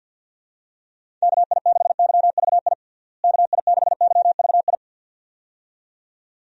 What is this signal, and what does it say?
Morse code, heard as a keyed tone, about 51 wpm, 700 Hz. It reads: KI6YVI KI6YVI
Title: USA Call Signs - 6 Characters - Repeated 2x - 50wpm